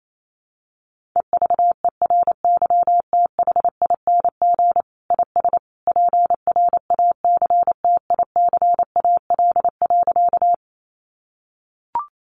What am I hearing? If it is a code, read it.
Morse code, heard as a keyed tone, about 28 wpm, 700 Hz. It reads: EVERYTHING IS PRACTICAL.
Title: Sentences from Top 1400 Words 28wpm